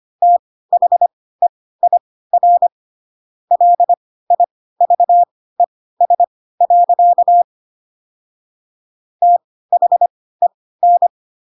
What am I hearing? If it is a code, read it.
Morse code, heard as a keyed tone, about 25 wpm, spaced as for 16 wpm, 700 Hz. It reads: THEIR LIVES. THEN
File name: Pinocchio / Pinocchio ch2 10 25 ik1hge